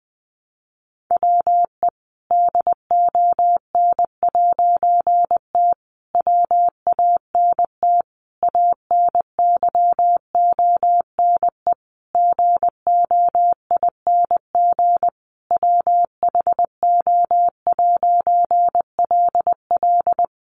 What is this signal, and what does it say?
Morse code, heard as a keyed tone, about 20 wpm, 700 Hz. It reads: WE DON'T WANT ANYONE GOING WHO'LL
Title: A Sound Of Thunder 20wpm